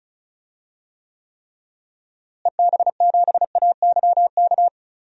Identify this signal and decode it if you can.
Morse code, heard as a keyed tone, about 36 wpm, 700 Hz. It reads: EB7AYK
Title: DXCC-100 Call Signs - 6 Characters - Repeated 3x - 35wpm